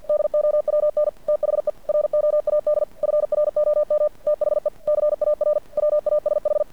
Morse code, heard as a keyed tone, about 50 wpm, 590 Hz. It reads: DOWN THE ROAD FROM THE CAR WASH
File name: archived / CRSnet